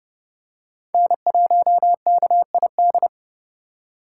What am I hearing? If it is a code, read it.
Morse code, heard as a keyed tone, about 30 wpm, 700 Hz. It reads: N1KID
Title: USA Call Signs - 5 Characters - 3x Character Spacing - 30wpm